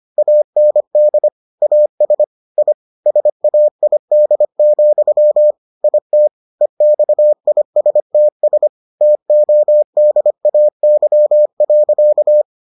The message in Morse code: AND AS I SAID, IT EXISTS TODAY.

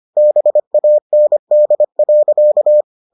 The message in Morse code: BAND.

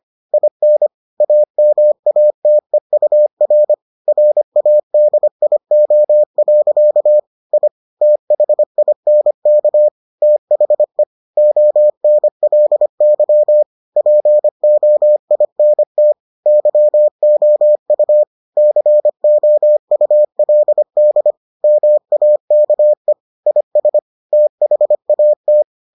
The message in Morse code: IN AMATEUR RADIO. I THINK THE ONLY POINT YOU COULD MAKE IS THAT

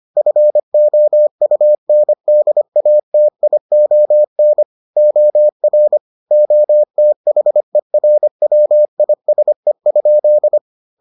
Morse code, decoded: FOUNDATION OR OTHERWISE?

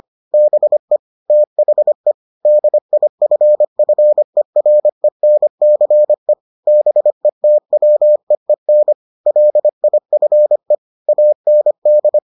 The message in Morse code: BE THE DIFFERENCE BETWEEN LIFE AND

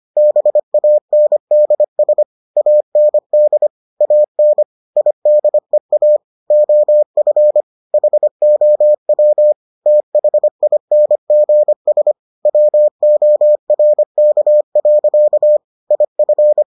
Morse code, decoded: BANDS AND AN IDEA OF HOW THINGS WORK. IF

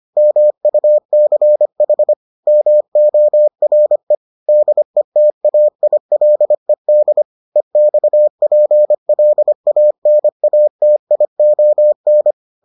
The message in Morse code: MUCH MORE DETAILED EXPLANATION

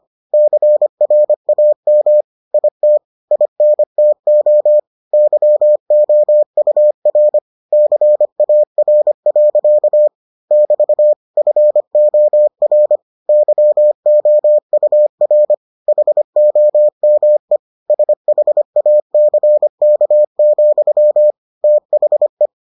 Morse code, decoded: CRAM IT INTO YOUR CAR. <BT> FOR YOUR HOME SHACK, THE

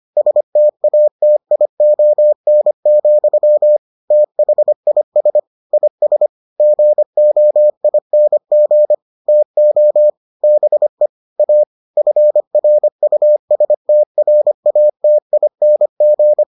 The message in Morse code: STATION, THIS IS GOING TO BE A FRUSTRATING

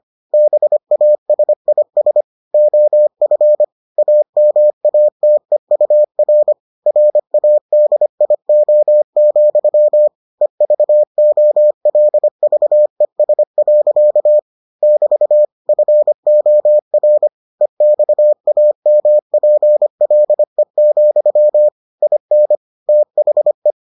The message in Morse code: BASIS OF AMATEUR RADIO, EVOLVES. <BT> FOR EXAMPLE, IN THE

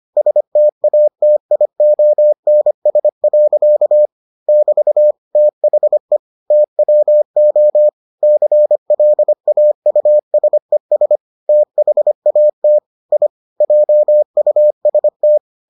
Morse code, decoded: STATIONS. <BT> THE TWO CLAUSES THAT I JUST

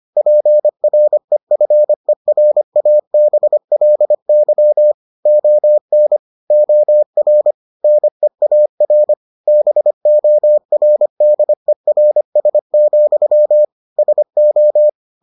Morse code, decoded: PREFERABLY ON OR NEAR BORDERS, SO

